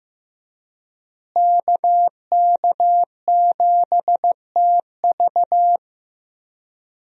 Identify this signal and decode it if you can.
Morse code, heard as a keyed tone, about 15 wpm, 700 Hz. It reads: KK7TV
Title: USA Call Signs - 5 Characters - Repeated 3x - 15wpm